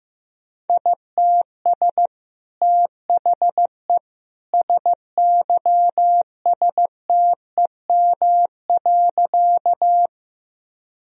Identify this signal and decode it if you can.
Morse code, heard as a keyed tone, about 15 wpm, 700 Hz. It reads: ITS THE SYSTEM.
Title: Sentences from Top 1000 Words - Review Part 3 - 15wpm